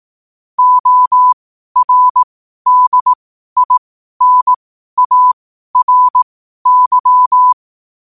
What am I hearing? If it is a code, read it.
Morse code, heard as a keyed tone, about 18 wpm, spaced as for 12 wpm, 1000 Hz. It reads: ORDINARY
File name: qotdfiles / 12wpm